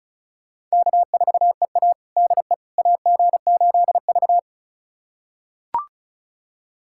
Morse code, heard as a keyed tone, about 36 wpm, 700 Hz. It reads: K4EA DE AG8V
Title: USA Callsign DE Callsign - 4 Characters - 35wpm